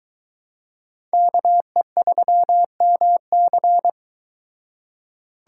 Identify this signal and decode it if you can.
Morse code, heard as a keyed tone, about 22 wpm, 700 Hz. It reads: KE3MC